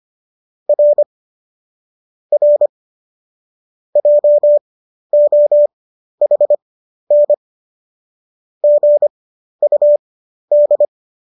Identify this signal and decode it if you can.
Morse code, heard as a keyed tone, about 25 wpm, spaced as for 12 wpm, 590 Hz. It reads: R R JOHN GUD